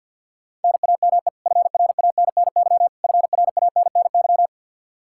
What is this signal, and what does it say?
Morse code, heard as a keyed tone, about 52 wpm, 700 Hz. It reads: NAME FRANNY FRANNY
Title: Practice QSO - Exchange 1 - Beginning - 50wpm